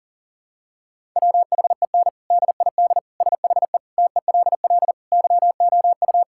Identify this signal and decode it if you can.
Morse code, heard as a keyed tone, about 40 wpm, 700 Hz. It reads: WHEN DID SHE TELL YOU
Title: Sentences from Top 1000 Words - Review Part 6 - 40wpm